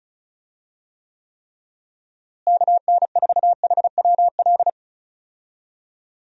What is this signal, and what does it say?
Morse code, heard as a keyed tone, about 34 wpm, 700 Hz. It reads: KN4HWL
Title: USA Call Signs - 6 Characters - Repeated 3x - 35wpm